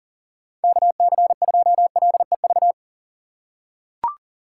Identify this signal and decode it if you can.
Morse code, heard as a keyed tone, about 40 wpm, 690 Hz. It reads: KC2LEV
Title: USA Call Signs - 4x Character Spacing - 40wpm